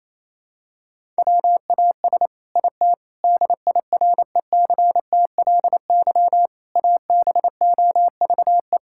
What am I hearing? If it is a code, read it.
Morse code, heard as a keyed tone, about 28 wpm, 700 Hz. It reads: WAS IT DIRECTLY ABOVE
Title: Sentences from Top 1000 Words - Review Part 5 - 28wpm